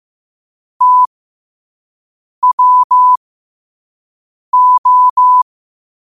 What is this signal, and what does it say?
Morse code, heard as a keyed tone, about 15 wpm, spaced as for 5 wpm, 1000 Hz. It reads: TWO
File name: qotdfiles / 5wpm